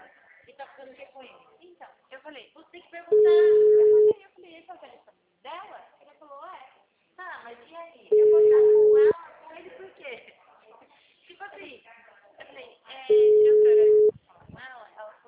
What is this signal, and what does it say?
Ringback tone (Europe, 425 Hz), with speech.